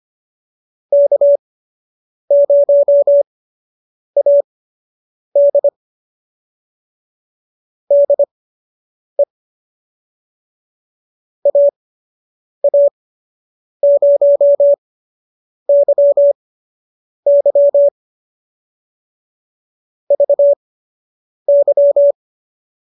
Morse code, heard as a keyed tone, about 25 wpm, spaced as for 8 wpm, 590 Hz. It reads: K0AD DE AA0YY VY